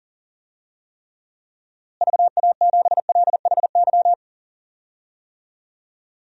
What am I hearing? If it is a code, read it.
Morse code, heard as a keyed tone, about 40 wpm, 700 Hz. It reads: VA7LHY
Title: DXCC-100 Call Signs - 6 Characters - Repeated 3x - 40wpm